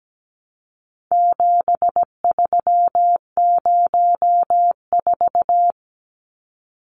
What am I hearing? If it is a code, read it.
Morse code, heard as a keyed tone, about 17 wpm, 700 Hz. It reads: 7304
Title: Numbers - 4-Digit 17wpm